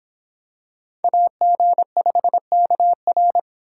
Morse code, heard as a keyed tone, about 26 wpm, 700 Hz. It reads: AG5KR